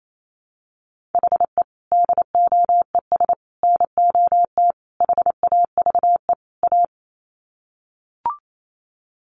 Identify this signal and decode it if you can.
Morse code, heard as a keyed tone, about 28 wpm, 700 Hz. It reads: HE DOES NOT HAVE A
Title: Sets of 5 Words Encore 28wpm